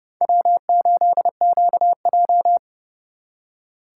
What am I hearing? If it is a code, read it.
Morse code, heard as a keyed tone, about 34 wpm, 700 Hz. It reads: W8QJ
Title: Speed Racing - USA 1x2 Call Signs 30wpm